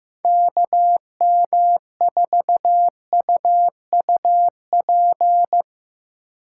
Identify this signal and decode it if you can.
Morse code, heard as a keyed tone, about 15 wpm, 700 Hz. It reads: KM4UUP
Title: USA Call Signs - 2x Character Spacing - 15wpm